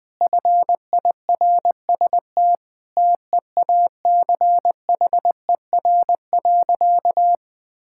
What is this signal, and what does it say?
Morse code, heard as a keyed tone, about 20 wpm, 700 Hz. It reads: FIRST TEACHER.
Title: Sentences from Top 1000 Words - Review Part 6 - 20wpm